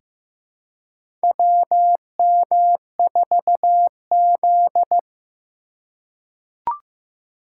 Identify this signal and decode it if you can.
Morse code, heard as a keyed tone, about 15 wpm, 700 Hz. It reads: WM4Z
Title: USA Call Signs - 4 Characters - 3x Character Spacing - 15wpm